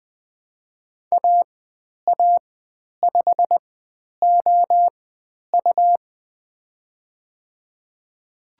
Morse code, heard as a keyed tone, about 20 wpm, spaced as for 10 wpm, 700 Hz. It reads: AA5OU
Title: 102 Lesson 17g - Letter u - Call Signs 20wpm